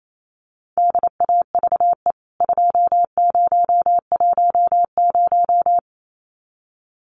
Morse code, heard as a keyed tone, about 28 wpm, 700 Hz. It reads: DAVE 2010
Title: CWT Exchange 28wpm